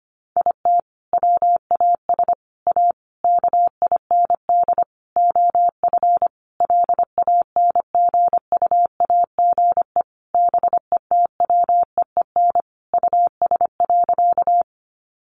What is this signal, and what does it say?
Morse code, heard as a keyed tone, about 25 wpm, 700 Hz. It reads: IT WAS A KIND OF LANGUAGE BETWEEN US.
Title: Sentences from Top 500 Words 25wpm